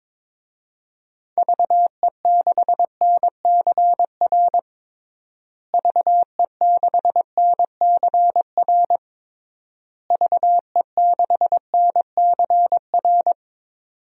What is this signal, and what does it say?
Morse code, heard as a keyed tone, about 22 wpm, 700 Hz. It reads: VE6NCR VE6NCR VE6NCR
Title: DXCC-100 Call Signs - 6 Characters - Repeated 3x - 22wpm